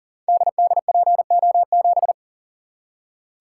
Morse code, heard as a keyed tone, about 42 wpm, 700 Hz. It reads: DDPO7
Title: Code Groups - 5 Letter:Numbers - 40wpm